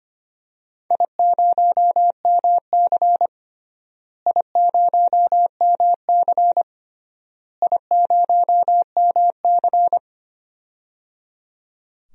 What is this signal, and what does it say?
Morse code, heard as a keyed tone, about 25 wpm, 700 Hz. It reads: I0MC I0MC I0MC
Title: DXCC-100 Call Signs - 4 Characters - Repeated 3x - 25wpm